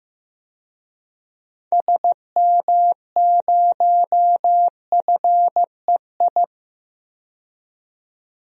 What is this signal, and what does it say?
Morse code, heard as a keyed tone, about 15 wpm, 700 Hz. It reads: SM0FEI
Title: DXCC-100 Call Signs - 6 Characters - 15wpm